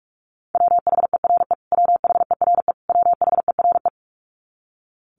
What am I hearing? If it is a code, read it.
Morse code, heard as a keyed tone, about 44 wpm, 700 Hz. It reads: WHERE WHERE WHERE
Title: Top 500 Words Repeated 45wpm